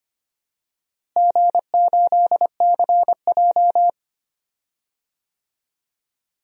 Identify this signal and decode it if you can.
Morse code, heard as a keyed tone, about 25 wpm, 700 Hz. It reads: G8CJ
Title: DXCC-100 Call Signs - 4 Characters - Repeated 3x - 25wpm